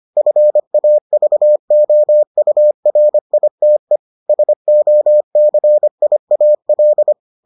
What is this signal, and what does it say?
Morse code, heard as a keyed tone, about 25 wpm, 600 Hz. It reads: FAVOURITE SOCIAL